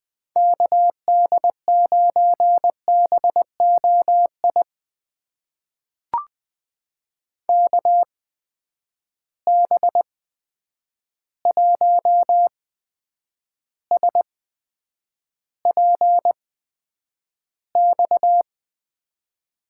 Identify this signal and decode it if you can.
Morse code, heard as a keyed tone, about 20 wpm, spaced as for 9 wpm, 700 Hz. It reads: KD9BOI KB1SPX
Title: USA Call Signs - 6 Characters - 8x Character Spacing - 20wpm